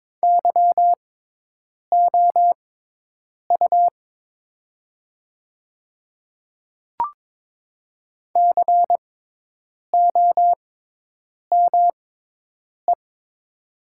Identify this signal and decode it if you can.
Morse code, heard as a keyed tone, about 22 wpm, spaced as for 8 wpm, 700 Hz. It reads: YOU COME
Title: 6x Character Spacing - Top 100 Words - 22wpm